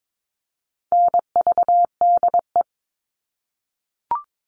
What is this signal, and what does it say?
Morse code, heard as a keyed tone, about 22 wpm, 700 Hz. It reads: NVDE
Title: All Cut Numbers - 4-Digit 22wpm